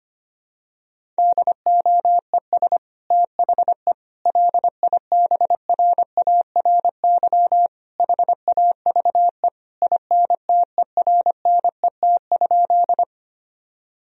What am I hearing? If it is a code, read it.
Morse code, heard as a keyed tone, about 25 wpm, 700 Hz. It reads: DOES THE LIBRARY HAVE INTERNET?
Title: Sentences from Top 1000 Words - Review Part 4 - 25wpm